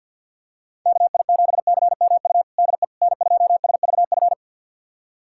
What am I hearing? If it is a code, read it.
Morse code, heard as a keyed tone, about 50 wpm, 700 Hz. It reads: KI7CMU DE N2SVF
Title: USA Callsign DE Callsign 50wpm